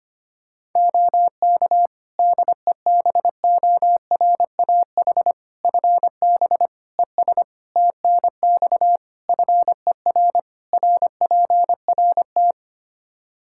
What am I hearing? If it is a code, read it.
Morse code, heard as a keyed tone, about 25 wpm, 700 Hz. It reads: OK DEBORAH FB ES TNX FER RPRT
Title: Practice QSO - Exchange 2 - Info - 25wpm